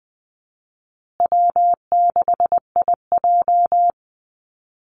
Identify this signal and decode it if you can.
Morse code, heard as a keyed tone, about 20 wpm, 700 Hz. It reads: W6IJ
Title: Easy Call Signs Repeated 2x - 20wpm